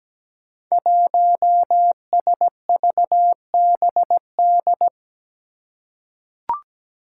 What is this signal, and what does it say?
Morse code, heard as a keyed tone, about 17 wpm, 700 Hz. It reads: 1SVBD
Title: Five Challenging Letter-Number Combinations 17wpm